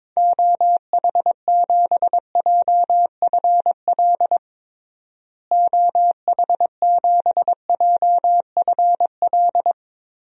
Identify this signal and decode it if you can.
Morse code, heard as a keyed tone, about 22 wpm, 700 Hz. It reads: OH7JFL OH7JFL
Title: DXCC-100 Call Signs - 6 Characters - Repeated 3x - 22wpm